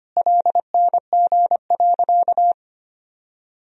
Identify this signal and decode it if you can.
Morse code, heard as a keyed tone, about 25 wpm, 700 Hz. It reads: LNG.
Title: Sentences from Top 100 Words and Abbreviations 25wpm